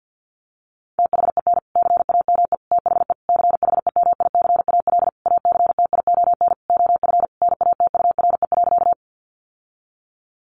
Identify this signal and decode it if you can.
Morse code, heard as a keyed tone, about 50 wpm, 700 Hz. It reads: THEN CAME THE CHEMICAL ACTION OF NATURE.
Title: Sentences from Top 1200 Words 50wpm